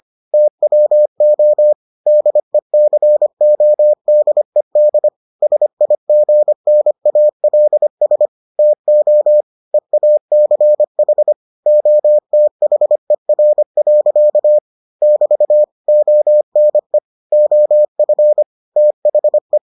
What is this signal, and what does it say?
Morse code, heard as a keyed tone, about 25 wpm, 600 Hz. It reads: TWO DECODED SIGNALS TO EACH OTHER. <BT> ONE OF THE